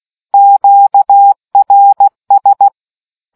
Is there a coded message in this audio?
Yes: Morse code from 0.3 to 2.7 s, heard as a keyed tone, about 16 wpm, 800 Hz. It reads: QRS